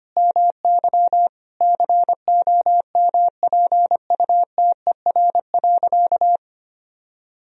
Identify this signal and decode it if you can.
Morse code, heard as a keyed tone, about 25 wpm, 700 Hz. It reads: MY COMPUTER.
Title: Sentences from Top 1000 Words - Review Part 3 - 25wpm